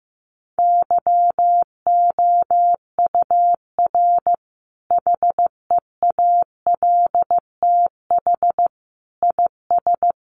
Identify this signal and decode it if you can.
Morse code, heard as a keyed tone, about 15 wpm, 700 Hz. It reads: YOUR HEALTH IS
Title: Sentences from Top 600 Words 15wpm